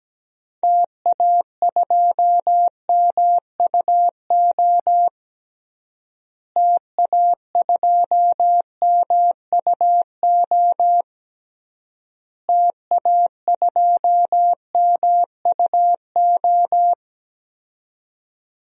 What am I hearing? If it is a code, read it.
Morse code, heard as a keyed tone, about 17 wpm, 700 Hz. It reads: TA2MUO TA2MUO TA2MUO
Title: DXCC-100 Call Signs - 6 Characters - Repeated 3x - 17wpm